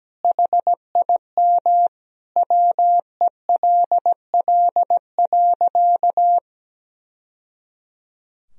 Morse code, heard as a keyed tone, about 17 wpm, 700 Hz. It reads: HIM WELL.
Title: Sentences from Top 1000 Words - Review Part 2 - 17wpm